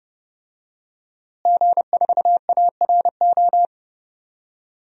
Morse code, heard as a keyed tone, about 30 wpm, 700 Hz. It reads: G4ARO
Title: DXCC-100 Call Signs - 5 Characters - Repeated 2x - 30wpm